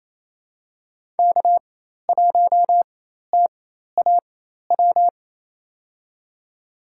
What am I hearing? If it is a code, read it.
Morse code, heard as a keyed tone, about 28 wpm, spaced as for 13 wpm, 700 Hz. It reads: K1TAW
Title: USA Call Signs - 5 Characters - 4x Character Spacing - 28wpm